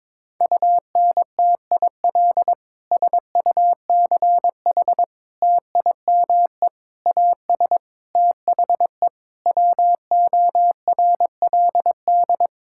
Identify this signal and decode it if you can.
Morse code, heard as a keyed tone, about 22 wpm, 700 Hz. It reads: UNTIL SUCH TIME AS THE WORLD
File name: Sentences with 9-Word Phrases - 22wpm